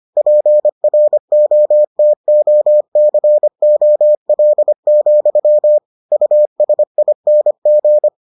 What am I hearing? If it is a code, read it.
Morse code, heard as a keyed tone, about 25 wpm, 600 Hz. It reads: PROTOCOL, USING